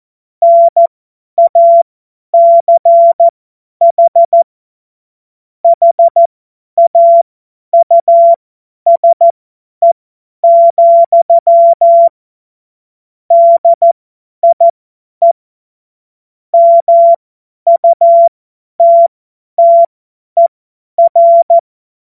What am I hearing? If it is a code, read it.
Morse code, heard as a keyed tone, about 14 wpm, spaced as for 10 wpm, 680 Hz. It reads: NACH HAUSE, DIE MUTTER
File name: Morsekurs / Witze / Witz 21. September 2021, 50 F70